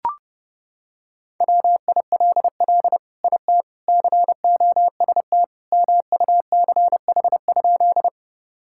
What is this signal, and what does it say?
Morse code, heard as a keyed tone, about 32 wpm, 700 Hz. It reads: WILL IT COST MUCH?
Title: Sentences from Top 1000 Words - Review Part 5 - 30wpm